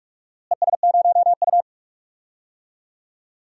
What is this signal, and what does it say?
Morse code, heard as a keyed tone, about 45 wpm, 700 Hz. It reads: ES0U
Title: DXCC-100 Call Signs - 4 Characters - Repeated 3x - 45wpm